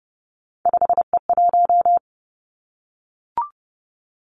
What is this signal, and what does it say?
Morse code, heard as a keyed tone, about 30 wpm, 700 Hz. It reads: 5E1
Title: Three Challenging Letter-Number Combinations 30wpm